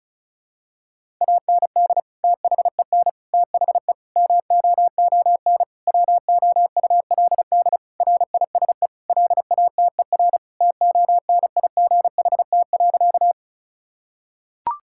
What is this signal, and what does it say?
Morse code, heard as a keyed tone, about 36 wpm, 700 Hz. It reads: AND THEN THE MOON WOULD RISE LATER TONIGHT.
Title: Sentences from Top 1000 Words - Review Part 6 - 35wpm